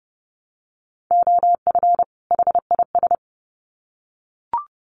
Morse code, heard as a keyed tone, about 30 wpm, 700 Hz. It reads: OF HIS
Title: Sets of 2 Words 30wpm